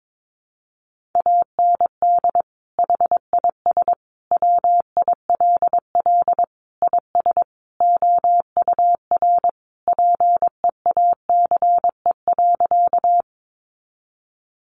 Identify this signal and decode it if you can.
Morse code, heard as a keyed tone, about 22 wpm, 700 Hz. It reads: AND HIS WILL IS OUR PEACE.
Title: Sentences from Top 1500 Words 22wpm